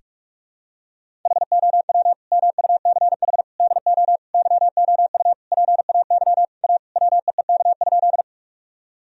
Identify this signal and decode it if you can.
Morse code, heard as a keyed tone, about 46 wpm, 700 Hz. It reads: HOW MUCH DO YOU PAY A WEEK?
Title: Sentences from Top 1000 Words - Review Part 2 - 45wpm